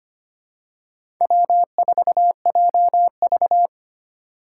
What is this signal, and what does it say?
Morse code, heard as a keyed tone, about 25 wpm, 700 Hz. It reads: W4JV
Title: USA Call Signs - 4 Characters - 2x Character Spacing - 25wpm